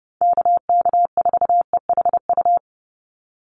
Morse code, heard as a keyed tone, about 30 wpm, 700 Hz. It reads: KK4EHU
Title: USA Call Signs 30wpm